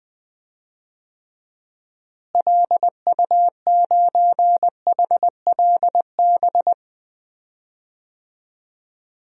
Morse code, heard as a keyed tone, about 20 wpm, 700 Hz. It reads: LU9HLB
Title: DXCC-100 Call Signs - 6 Characters - Repeated 2x - 20wpm